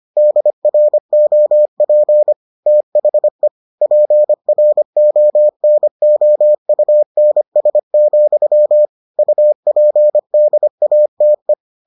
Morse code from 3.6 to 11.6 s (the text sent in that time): PRONOUNS, UPDATE